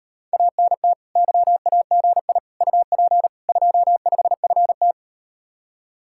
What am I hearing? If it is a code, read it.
Morse code, heard as a keyed tone, about 38 wpm, 700 Hz. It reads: ANT YAGI UP 25FT